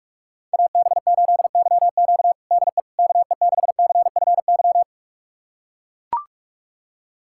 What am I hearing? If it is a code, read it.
Morse code, heard as a keyed tone, about 42 wpm, 700 Hz. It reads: AB8YQ DE KE6CFY
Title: USA Callsign DE Callsign 45wpm